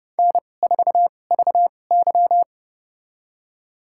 Morse code, heard as a keyed tone, about 30 wpm, spaced as for 21 wpm, 700 Hz. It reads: N4VY
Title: USA Call Signs - 4 Characters - 2x Character Spacing - 30wpm